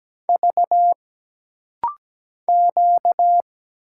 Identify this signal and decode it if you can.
Morse code, heard as a keyed tone, about 17 wpm, 700 Hz. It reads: V Q
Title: Single Letters - Rapid-Fire 17wpm